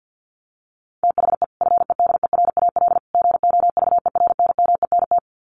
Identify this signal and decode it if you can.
Morse code, heard as a keyed tone, about 50 wpm, 700 Hz. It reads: THE FEDERAL GOVERNMENT
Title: Sets of 3 Words 50wpm